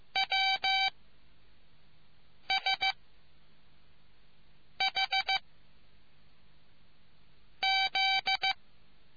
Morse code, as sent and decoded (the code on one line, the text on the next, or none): .-- / ... / .... / --..
W S H Z